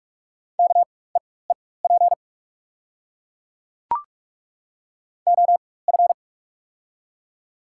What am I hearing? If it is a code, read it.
Morse code, heard as a keyed tone, about 43 wpm, spaced as for 21 wpm, 690 Hz. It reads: KEEP OF
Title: 4x Character Spacing - Top 300 Words - 45wpm